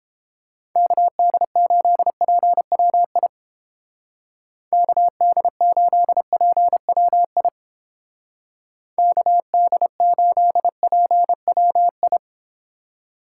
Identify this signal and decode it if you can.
Morse code, heard as a keyed tone, about 29 wpm, 700 Hz. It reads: KD8PWI KD8PWI KD8PWI